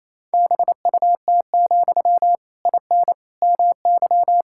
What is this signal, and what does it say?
Morse code, heard as a keyed tone, about 28 wpm, 700 Hz. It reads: BUT, IN MY
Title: Sentences from Top 1000 Words - Review Part 5 - 28wpm